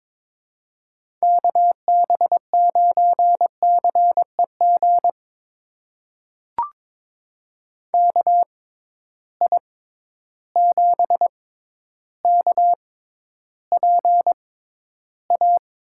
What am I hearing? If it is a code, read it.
Morse code, heard as a keyed tone, about 22 wpm, spaced as for 11 wpm, 700 Hz. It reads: KB9CEG KI7KPA
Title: USA Call Signs - 6 Characters - 6x Character Spacing - 22wpm